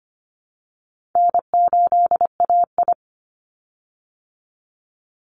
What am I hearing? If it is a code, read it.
Morse code, heard as a keyed tone, about 25 wpm, 700 Hz. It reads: N8AI
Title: Easy Call Signs Repeated 2x - 25wpm